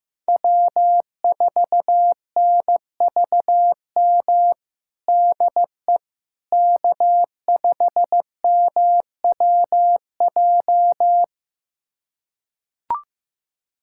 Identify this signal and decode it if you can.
Morse code, heard as a keyed tone, about 15 wpm, 700 Hz. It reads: W4NVM DE K5MWJ
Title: USA Callsign DE Callsign - 5 Characters - 15wpm